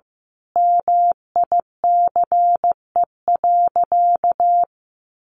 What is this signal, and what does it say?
Morse code, heard as a keyed tone, about 15 wpm, 700 Hz. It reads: MICE.
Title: A Sound Of Thunder 15wpm